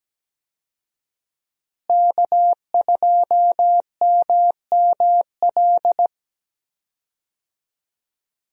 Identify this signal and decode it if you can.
Morse code, heard as a keyed tone, about 17 wpm, 700 Hz. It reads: K2MML
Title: USA Call Signs - 5 Characters - Repeated 3x - 17wpm